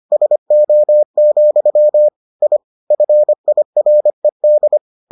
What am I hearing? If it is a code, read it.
Morse code, heard as a keyed tone, about 25 wpm, 600 Hz. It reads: SO, I FIRED